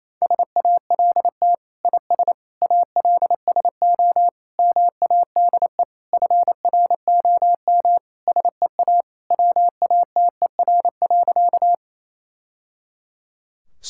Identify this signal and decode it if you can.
Morse code, heard as a keyed tone, about 28 wpm, 700 Hz. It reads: SALT IS ALSO MADE FROM SEA WATER.
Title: Sentences from Top 1000 Words - Review Part 5 - 28wpm